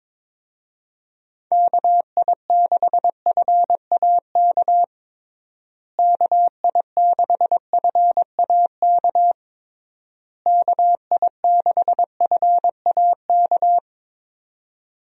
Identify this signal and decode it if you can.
Morse code, heard as a keyed tone, about 22 wpm, 700 Hz. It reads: KI6FAK KI6FAK KI6FAK
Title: USA Call Signs - 6 Characters - Repeated 3x - 22wpm